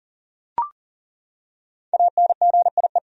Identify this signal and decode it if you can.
Morse code, heard as a keyed tone, about 40 wpm, 700 Hz. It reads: ANGIE
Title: USA Common Names - 40wpm